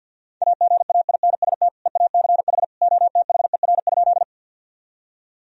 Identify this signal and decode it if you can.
Morse code, heard as a keyed tone, about 52 wpm, 700 Hz. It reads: AGAINST EACH OTHER?
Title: Sentences from Top 1000 Words - Review Part 1 - 50wpm